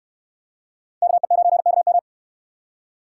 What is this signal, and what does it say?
Morse code, heard as a keyed tone, about 69 wpm, 700 Hz. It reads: KE9KM